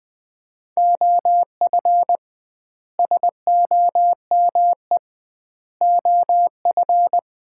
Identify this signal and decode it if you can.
Morse code, heard as a keyed tone, about 21 wpm, 700 Hz. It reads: OF SOME OF
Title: 2x Word-Spacing - Sets of 4 Words - 20wpm